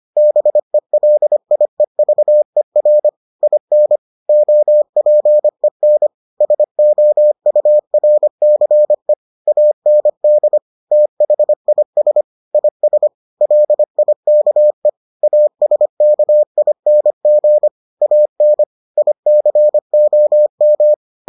Morse code, heard as a keyed tone, about 25 wpm, 600 Hz. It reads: BELIEVER IN OPEN SOURCE AND THIS IS LIKE ASKING AN ICOM